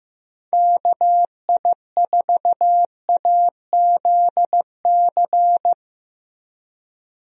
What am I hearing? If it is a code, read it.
Morse code, heard as a keyed tone, about 15 wpm, 700 Hz. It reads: KI4AZC
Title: USA Call Signs - 6 Characters - Repeated 3x - 15wpm